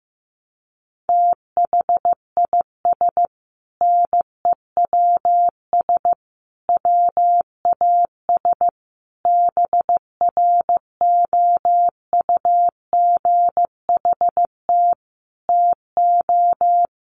Morse code, heard as a keyed tone, about 15 wpm, 700 Hz. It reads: THIS NEWS WAS BROUGHT TO
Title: Sentences from Top 800 Words 15wpm